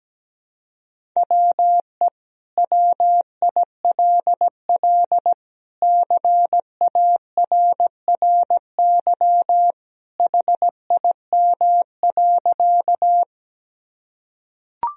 Morse code, heard as a keyed tone, about 17 wpm, 700 Hz. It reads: WE WILL CARRY HIM.
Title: Sentences from Top 1000 Words - Review Part 1 - 17wpm